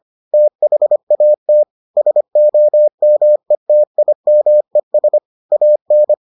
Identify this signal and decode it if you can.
Morse code, heard as a keyed tone, about 25 wpm, 600 Hz. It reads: THAT SOMETIMES AN